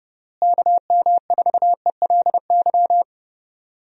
Morse code, heard as a keyed tone, about 32 wpm, 700 Hz. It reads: KM4ELY
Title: USA Call Signs - 6 Characters - Repeated 2x - 30wpm